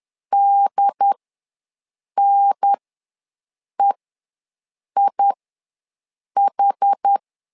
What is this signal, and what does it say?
Morse code, heard as a keyed tone, about 10 wpm, spaced as for 6 wpm, 800 Hz. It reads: DNEIH
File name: LEZIONI / 2 UHDN / 2 UHDN ws 10wpm 2044